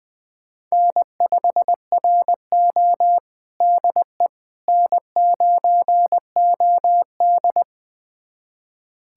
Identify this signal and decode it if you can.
Morse code, heard as a keyed tone, about 21 wpm, 700 Hz. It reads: N5RO DE N9OD
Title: USA Callsign DE Callsign - 4 Characters - 20wpm